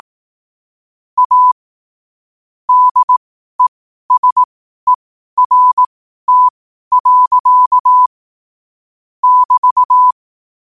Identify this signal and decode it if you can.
Morse code, heard as a keyed tone, about 18 wpm, spaced as for 12 wpm, 1000 Hz. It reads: A DESERT. <BT>
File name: qotdfiles / 12wpm